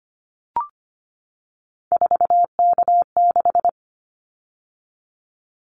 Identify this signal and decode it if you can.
Morse code, heard as a keyed tone, about 25 wpm, 700 Hz. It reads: 4K6
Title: Callsign Prefixes for Top 100 DXCC - 25wpm